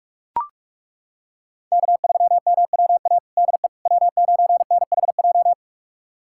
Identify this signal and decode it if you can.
Morse code, heard as a keyed tone, about 41 wpm, 700 Hz. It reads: K3MWA DE W9NHJ
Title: USA Callsign DE Callsign - 5 Characters - 45wpm